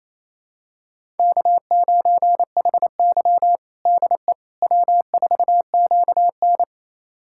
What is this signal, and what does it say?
Morse code, heard as a keyed tone, about 28 wpm, 700 Hz. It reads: K9HY DE W4QN
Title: USA Callsign DE Callsign - 4 Characters - 28wpm